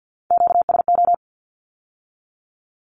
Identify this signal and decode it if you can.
Morse code, heard as a keyed tone, about 50 wpm, 700 Hz. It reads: QSO
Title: Q-Codes with Spoken Meaning - 50wpm